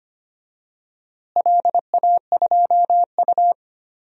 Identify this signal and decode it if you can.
Morse code, heard as a keyed tone, about 25 wpm, 700 Hz. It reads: LA2U
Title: DXCC-100 Call Signs - 4 Characters - Repeated 2x - 25wpm